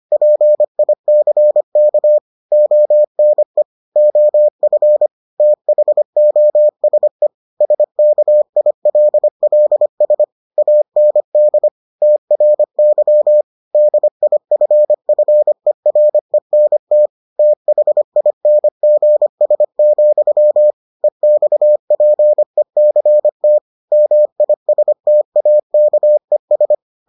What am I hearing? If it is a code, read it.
Morse code, heard as a keyed tone, about 25 wpm, 600 Hz. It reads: PICK ONE OF THOSE SKILLS AND TRY DIFFERENT THINGS, EXPECT MISTAKES